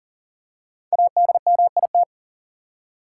Morse code, heard as a keyed tone, about 40 wpm, 700 Hz. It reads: ADMIT
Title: Five Letter Words - 40wpm